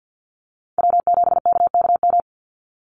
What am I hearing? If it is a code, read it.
Morse code, heard as a keyed tone, about 49 wpm, 700 Hz. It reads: W7KKM
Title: NA SOTA Chaser Callsigns 50wpm